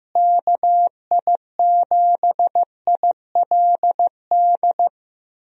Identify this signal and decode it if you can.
Morse code, heard as a keyed tone, about 15 wpm, 700 Hz. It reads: KI7ILD
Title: USA Call Signs - 6 Characters - Repeated 3x - 15wpm